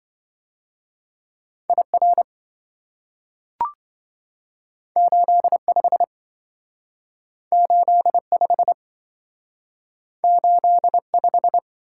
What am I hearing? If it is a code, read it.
Morse code, heard as a keyed tone, about 27 wpm, 700 Hz. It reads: IR 85 85 85